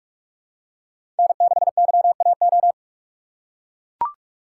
Morse code, heard as a keyed tone, about 42 wpm, 700 Hz. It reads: N6YAO
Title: USA Call Signs - 5 Characters - 4x Character Spacing - 45wpm